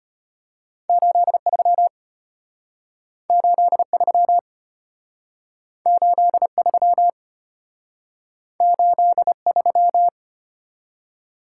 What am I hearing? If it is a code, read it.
Morse code, heard as a keyed tone, about 34 wpm, 700 Hz. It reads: 83 83 83 83